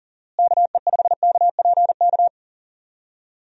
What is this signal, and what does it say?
Morse code, heard as a keyed tone, about 40 wpm, 700 Hz. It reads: KE5KPK
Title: USA Call Signs - 6 Characters - Repeated 2x - 40wpm